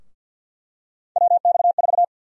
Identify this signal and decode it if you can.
Morse code, heard as a keyed tone, about 51 wpm, 700 Hz. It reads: WX4
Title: USA Call Sign Prefixes - 50wpm